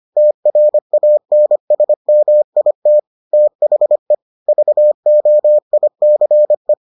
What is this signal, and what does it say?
Morse code, heard as a keyed tone, about 25 wpm, 600 Hz. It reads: TRANSMIT THE VOICE